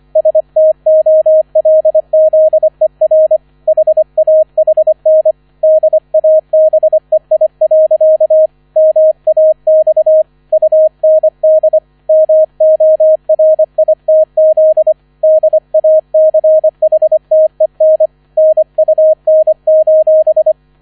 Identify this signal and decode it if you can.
Morse code, heard as a keyed tone, about 25 wpm, 630 Hz, with radio noise. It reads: STOLZER HAHN DABEI. MAX UND MORITZ DACHTEN NUN:
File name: 001 Erster Streich 25wpm